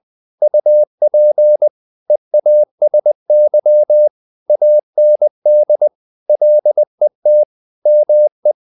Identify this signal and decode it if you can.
Morse code, heard as a keyed tone, about 20 wpm, 600 Hz. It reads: UP EASY AND LET ME
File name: the old man and the sea / the old man and the sea part 1 20wpm